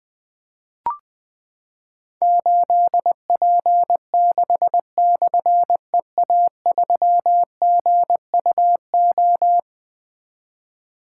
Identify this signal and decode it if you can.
Morse code, heard as a keyed tone, about 21 wpm, 700 Hz. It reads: 8P6/EA3GUO
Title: Difficult Call Signs 20wpm